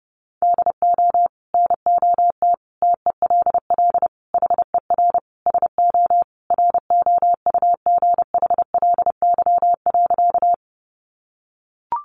Morse code, heard as a keyed tone, about 30 wpm, 700 Hz. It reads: DO NOT TELL HER SO ROUGHLY.
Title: Sentences from Top 1500 Words 30wpm